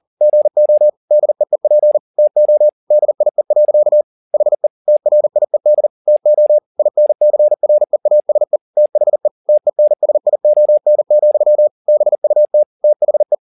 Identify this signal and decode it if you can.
Morse code, heard as a keyed tone, about 40 wpm, 600 Hz. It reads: GO DEEP TO DIE. HE TRIED TO INCREASE THE TENSION, BUT THE